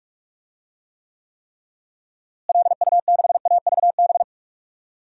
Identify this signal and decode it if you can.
Morse code, heard as a keyed tone, about 42 wpm, 700 Hz. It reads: LU6AVB
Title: DXCC-100 Call Signs - 6 Characters - Repeated 3x - 45wpm